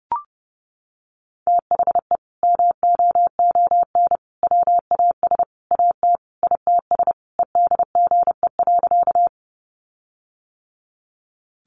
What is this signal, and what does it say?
Morse code, heard as a keyed tone, about 30 wpm, 700 Hz. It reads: THE MOON WAS AT ITS EDGE.
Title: Sentences from Top 1300 Words 30wpm